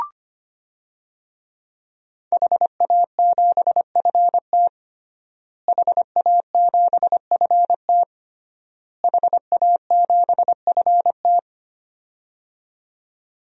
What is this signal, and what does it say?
Morse code, heard as a keyed tone, about 25 wpm, 700 Hz. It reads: HA7FT HA7FT HA7FT
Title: DXCC-100 Call Signs - 5 Characters - Repeated 3x - 25wpm